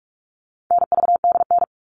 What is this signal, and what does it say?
Morse code, heard as a keyed tone, about 44 wpm, 700 Hz. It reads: NVDN
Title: All Cut Numbers - 4-Digit 45wpm